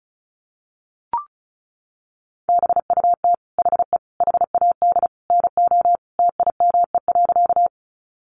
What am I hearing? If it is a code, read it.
Morse code, heard as a keyed tone, about 34 wpm, 700 Hz. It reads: BUT HE HAD NO TIME.
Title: Sentences from Top 100 Words 35wpm